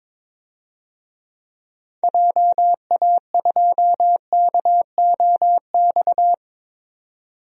Morse code, heard as a keyed tone, about 22 wpm, 700 Hz. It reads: JA2KOX
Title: DXCC-100 Call Signs - 6 Characters - 22wpm